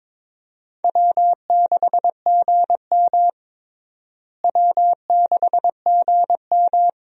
Morse code, heard as a keyed tone, about 22 wpm, 700 Hz. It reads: W6GM W6GM
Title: USA Call Signs - 4 Characters - Repeated 2x - 22wpm